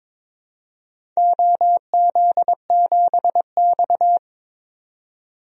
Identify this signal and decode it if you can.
Morse code, heard as a keyed tone, about 22 wpm, 700 Hz. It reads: OZ7X
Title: DXCC-100 Call Signs - 4 Characters - 22wpm